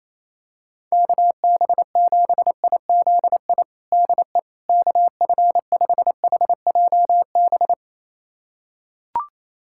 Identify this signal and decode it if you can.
Morse code, heard as a keyed tone, about 27 wpm, 700 Hz. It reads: KB7IZI DE KF5HJB
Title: USA Callsign DE Callsign - 6 Characters - 28wpm